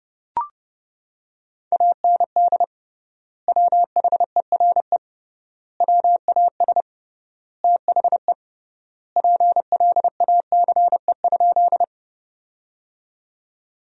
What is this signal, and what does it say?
Morse code, heard as a keyed tone, about 30 wpm, 700 Hz. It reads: AND WHERE WAS THE PLACE?
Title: 3x Word-Spacing - Sentences from Top 200 Words - 30wpm